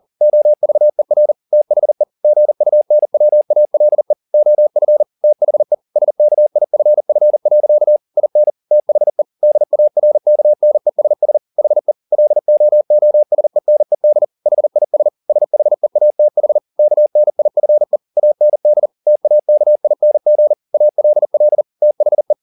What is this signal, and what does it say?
Morse code, heard as a keyed tone, about 40 wpm, 600 Hz. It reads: OVER THE GUNWALE OF THE SKIFF. IN THE DARKNESS HE LOOSENED HIS SHEATH KNIFE AND TAKING ALL THE